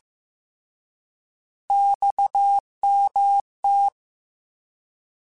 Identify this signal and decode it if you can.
Morse code, heard as a keyed tone, about 15 wpm, 800 Hz. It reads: XMT
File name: Abreviations / Abreviations mp3s / Abreviations@13